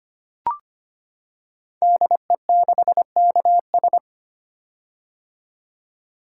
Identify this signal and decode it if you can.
Morse code, heard as a keyed tone, about 25 wpm, 690 Hz. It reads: DE6KS
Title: Code Groups - 5 Letter:Numbers - 25wpm